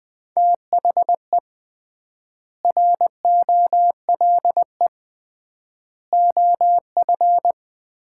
Morse code, heard as a keyed tone, about 20 wpm, 700 Hz. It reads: THE ROLE OF
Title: 3x Word-Spacing - Sets of 4 Words - 20wpm